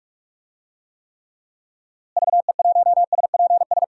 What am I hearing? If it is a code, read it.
Morse code, heard as a keyed tone, about 46 wpm, 700 Hz. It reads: VE1SPS
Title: DXCC-100 Call Signs - 6 Characters - Repeated 2x - 45wpm